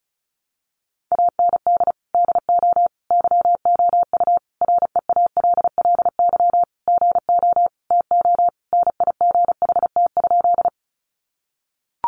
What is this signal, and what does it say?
Morse code, heard as a keyed tone, about 34 wpm, 700 Hz. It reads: AND DO YOU REALLY GO TONIGHT?
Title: Sentences from Top 600 Words 35wpm